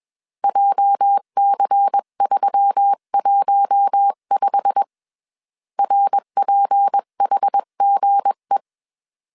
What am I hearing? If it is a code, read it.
Morse code, heard as a keyed tone, about 21 wpm, 800 Hz. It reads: JC315 RPHGE